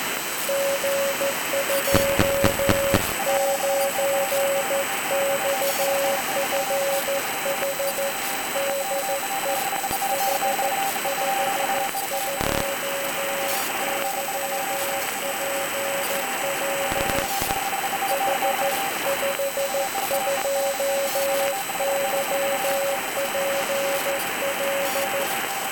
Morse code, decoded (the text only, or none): G29XFH DE H52YPL H52YPL